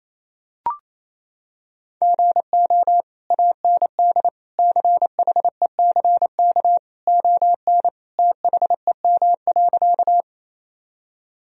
Morse code, heard as a keyed tone, about 28 wpm, 700 Hz. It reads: GO AND CHECK ON THEM.
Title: Sentences from Top 1000 Words - Review Part 3 - 28wpm